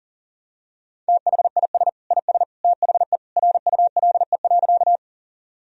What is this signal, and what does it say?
Morse code, heard as a keyed tone, about 39 wpm, 700 Hz. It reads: THIS IS THE RULE.
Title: Sentences from Top 1000 Words - Review Part 2 - 40wpm